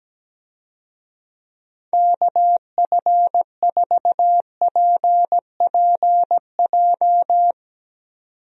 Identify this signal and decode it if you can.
Morse code, heard as a keyed tone, about 17 wpm, 700 Hz. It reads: KF4PPJ
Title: USA Call Signs - 6 Characters - 17wpm